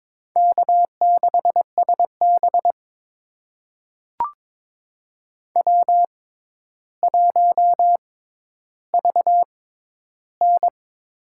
Morse code, heard as a keyed tone, about 22 wpm, spaced as for 11 wpm, 700 Hz. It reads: K6SB W1VN
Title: USA Call Signs - 4 Characters - 6x Character Spacing - 22wpm